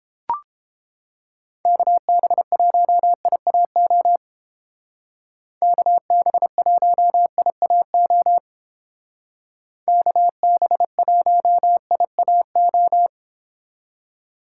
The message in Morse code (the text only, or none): KB1IAO KB1IAO KB1IAO